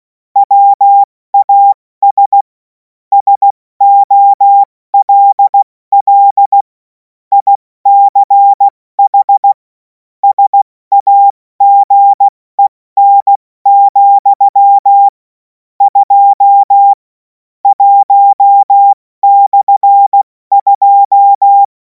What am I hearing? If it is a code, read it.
Morse code, heard as a keyed tone, about 16 wpm, 800 Hz. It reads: WAS SOLL ICH SAGEN, 2 1/2
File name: Morsekurs / Witze / Witz 11. August 2026 70 F80